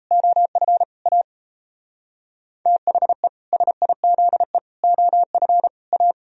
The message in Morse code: OF A THE SIZE OF A

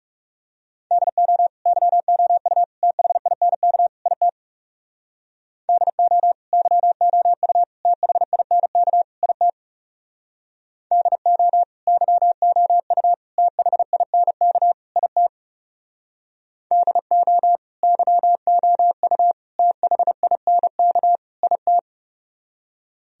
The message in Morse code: DO YOU THINK IT DO YOU THINK IT DO YOU THINK IT DO YOU THINK IT

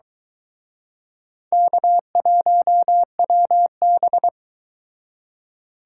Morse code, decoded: K1WB

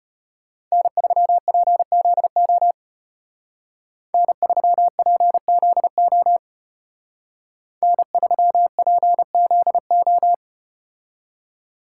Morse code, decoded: N3PZO N3PZO N3PZO

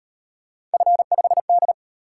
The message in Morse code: F5D

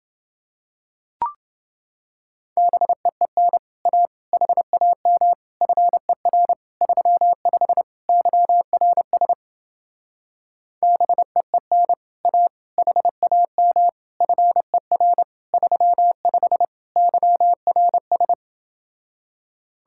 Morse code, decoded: BEEN A HAM FER 35 YRS BEEN A HAM FER 35 YRS

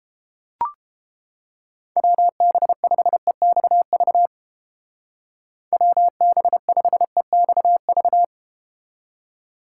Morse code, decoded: WB5EXV WB5EXV

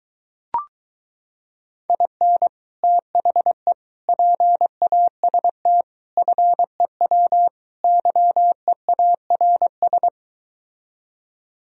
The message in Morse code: IN THE PAST FEW YEARS

